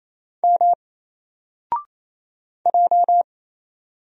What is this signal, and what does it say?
Morse code, heard as a keyed tone, about 27 wpm, 700 Hz. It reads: M J
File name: Single-Letter-Number-28wpm